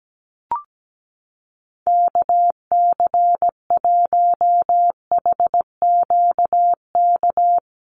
Morse code, heard as a keyed tone, about 17 wpm, 700 Hz. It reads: KC1HQK
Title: USA Call Signs Repeated 3x 17wpm